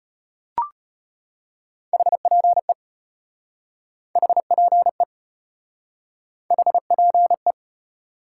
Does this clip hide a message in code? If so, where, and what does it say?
Yes: Morse code from 1.9 to 7.5 s, heard as a keyed tone, about 33 wpm, 700 Hz. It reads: HPE HPE HPE